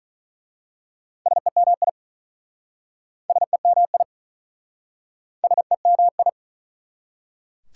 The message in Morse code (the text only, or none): SEMI SEMI SEMI